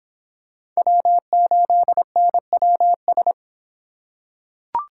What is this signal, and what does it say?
Morse code, heard as a keyed tone, about 26 wpm, 700 Hz. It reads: W8NWS